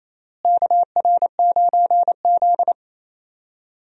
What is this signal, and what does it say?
Morse code, heard as a keyed tone, about 28 wpm, 700 Hz. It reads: KR9Z
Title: USA Call Signs - 4 Characters - Repeated 3x - 28wpm